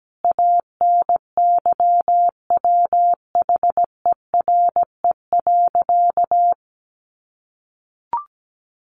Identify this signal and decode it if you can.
Morse code, heard as a keyed tone, about 17 wpm, 700 Hz. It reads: ANYWHERE.
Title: Sentences from Top 1400 Words 17wpm